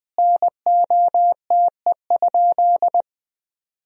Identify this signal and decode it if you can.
Morse code, heard as a keyed tone, about 20 wpm, 700 Hz. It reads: NOTE?
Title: 4x Word-Spacing - Sentences from Top 500 Words - 20wpm